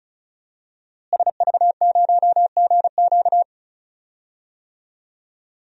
Morse code, heard as a keyed tone, about 36 wpm, 700 Hz. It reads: SV0GQ
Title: DXCC-100 Call Signs - 5 Characters - Repeated 3x - 35wpm